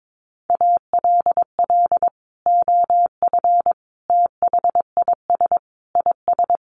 Morse code, heard as a keyed tone, about 22 wpm, 700 Hz. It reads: ALL OF THIS IS
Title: Sentences from Top 1100 Words 22wpm